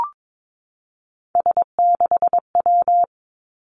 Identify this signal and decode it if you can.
Morse code, heard as a keyed tone, about 22 wpm, 700 Hz. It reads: S6W
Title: Three Challenging Letter-Number Combinations 22wpm